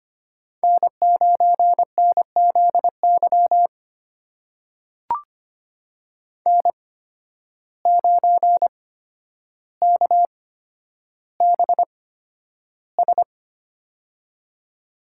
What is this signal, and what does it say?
Morse code, heard as a keyed tone, about 25 wpm, spaced as for 11 wpm, 700 Hz. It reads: N9NZY N9KBS
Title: USA Call Signs - 5 Characters - 8x Character Spacing - 25wpm